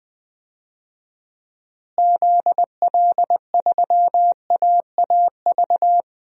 Morse code, heard as a keyed tone, about 20 wpm, 690 Hz. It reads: ZL3AAV
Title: DXCC-100 Call Signs - 6 Characters - Repeated 2x - 20wpm